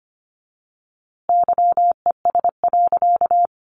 Morse code, heard as a keyed tone, about 25 wpm, 700 Hz. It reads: YES.
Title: A Sound Of Thunder 25wpm